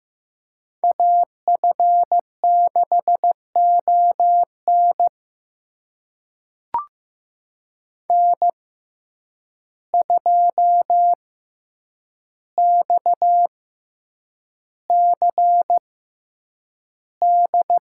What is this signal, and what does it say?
Morse code, heard as a keyed tone, about 15 wpm, spaced as for 8 wpm, 700 Hz. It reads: AF6ON N2XCD
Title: USA Call Signs - 5 Characters - 6x Character Spacing - 15wpm